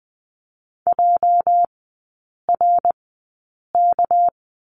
Morse code, heard as a keyed tone, about 20 wpm, spaced as for 8 wpm, 700 Hz. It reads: JRK
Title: 205 Lesson 33e - Letter j - Three Characters 20wpm